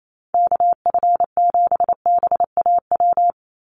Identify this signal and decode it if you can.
Morse code, heard as a keyed tone, about 28 wpm, 700 Hz. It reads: KF7BAW
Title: USA Call Signs 28wpm